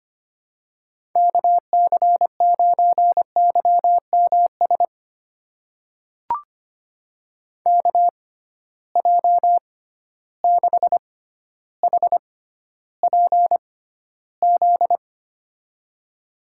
Morse code, heard as a keyed tone, about 25 wpm, spaced as for 13 wpm, 700 Hz. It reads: KC9YMS KJ6HPZ
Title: USA Call Signs - 6 Characters - 6x Character Spacing - 25wpm